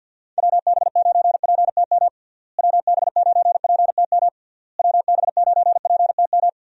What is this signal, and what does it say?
Morse code, heard as a keyed tone, about 50 wpm, 700 Hz. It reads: WB9PTM WB9PTM WB9PTM
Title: USA Call Signs - 6 Characters - Repeated 3x - 50wpm